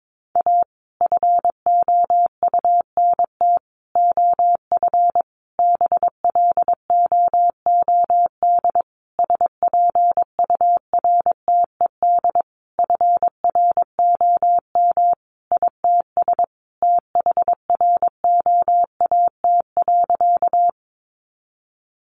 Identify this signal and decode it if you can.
Morse code, heard as a keyed tone, about 22 wpm, 700 Hz. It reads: A FOUNT OF BLOOD SPURTED FROM ITS THROAT.
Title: A Sound Of Thunder 22wpm